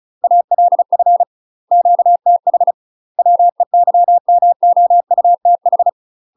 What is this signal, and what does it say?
Morse code, heard as a keyed tone, about 36 wpm, 700 Hz. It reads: ALF QTH WEYMOUTH
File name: UK MASTER QSOs TRAINING / 35WPM / UK MASTER QSOs 0026